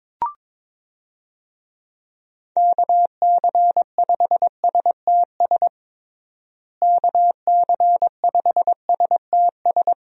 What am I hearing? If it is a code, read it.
Morse code, heard as a keyed tone, about 22 wpm, 700 Hz. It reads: KC5STS KC5STS
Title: USA Call Signs - 6 Characters - Repeated 3x - 22wpm